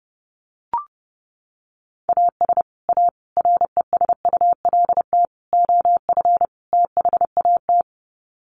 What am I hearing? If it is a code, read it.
Morse code, heard as a keyed tone, about 28 wpm, 700 Hz. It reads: AS A RESULT OF THAT
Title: Sets of 5 Words Encore 30wpm